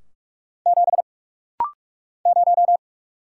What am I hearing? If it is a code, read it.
Morse code, heard as a keyed tone, about 44 wpm, 700 Hz. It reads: 7 0
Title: Single Letter-Number - Rapid-Fire 45wpm